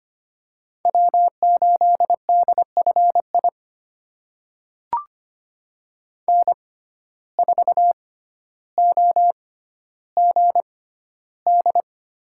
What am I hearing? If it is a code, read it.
Morse code, heard as a keyed tone, about 25 wpm, spaced as for 13 wpm, 700 Hz. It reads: W8DFI N4OGD
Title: USA Call Signs - 5 Characters - 6x Character Spacing - 25wpm